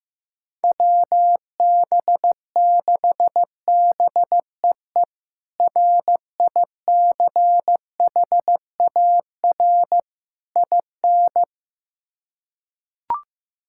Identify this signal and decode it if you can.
Morse code, heard as a keyed tone, about 16 wpm, 700 Hz. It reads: WB6BEE RICHAR IN
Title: K1USN SST QSO Practice - 15wpm